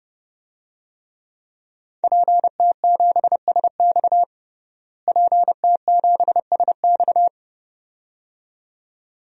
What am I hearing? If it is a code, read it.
Morse code, heard as a keyed tone, about 30 wpm, 700 Hz. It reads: PT7SX PT7SX
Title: DXCC-100 Call Signs - 5 Characters - Repeated 2x - 30wpm